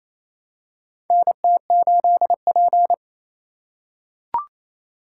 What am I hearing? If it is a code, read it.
Morse code, heard as a keyed tone, about 27 wpm, 700 Hz. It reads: NT8P
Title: USA Call Signs - 4 Characters - 3x Character Spacing - 28wpm